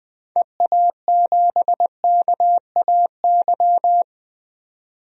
Morse code, heard as a keyed tone, about 20 wpm, 700 Hz. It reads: EA7KAY
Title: DXCC-100 Call Signs - 6 Characters - 20wpm